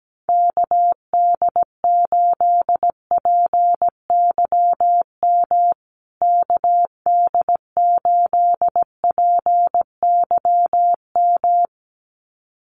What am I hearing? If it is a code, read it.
Morse code, heard as a keyed tone, about 17 wpm, 700 Hz. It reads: KD8PYM KD8PYM
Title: USA Call Signs Repeated 2x 17wpm